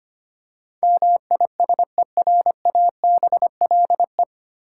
Morse code, heard as a keyed tone, about 25 wpm, 690 Hz. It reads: MISERABLE
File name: practice / 25wpm / 9 Letter Words - First 3 Letters Given - 25wpm